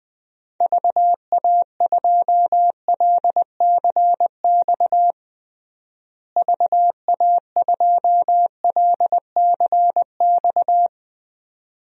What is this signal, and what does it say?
Morse code, heard as a keyed tone, about 20 wpm, 700 Hz. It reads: VA2LCX VA2LCX
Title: DXCC-100 Call Signs - 6 Characters - Repeated 2x - 20wpm